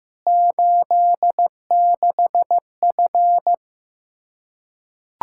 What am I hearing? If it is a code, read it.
Morse code, heard as a keyed tone, about 14 wpm, 700 Hz. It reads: 86F
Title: Practice QSO - Exchange 2 - Info - 15wpm